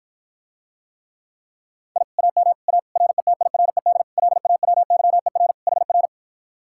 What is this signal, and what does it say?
Morse code, heard as a keyed tone, about 54 wpm, 700 Hz. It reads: I AM A RETIRED LAWYER HR